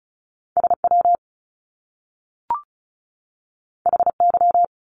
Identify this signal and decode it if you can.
Morse code, heard as a keyed tone, about 36 wpm, 700 Hz. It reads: SW HY
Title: Top Two Letter Combinations in Words 35wpm